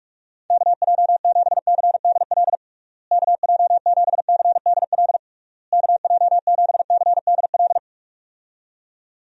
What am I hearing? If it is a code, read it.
Morse code, heard as a keyed tone, about 48 wpm, 700 Hz. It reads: KJ7CDL KJ7CDL KJ7CDL
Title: USA Call Signs - 6 Characters - Repeated 3x - 45wpm